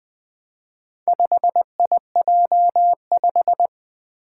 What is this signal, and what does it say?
Morse code, heard as a keyed tone, about 20 wpm, 700 Hz. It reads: 5IJ5
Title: Four Challenging Letter-Number Combinations 20wpm